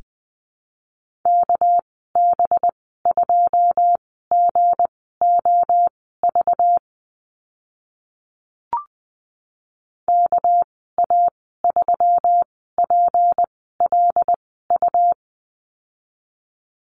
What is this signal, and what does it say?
Morse code, heard as a keyed tone, about 20 wpm, spaced as for 15 wpm, 700 Hz. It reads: KB2GOV KA3PLU
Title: 200 Lesson 32g - Letter k - Call Signs 20wpm